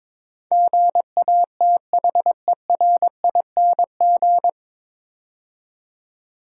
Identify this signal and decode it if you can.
Morse code, heard as a keyed tone, about 22 wpm, 700 Hz. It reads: GATHERING
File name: 9 Letter Words - First 3 Letters Given - 22wpm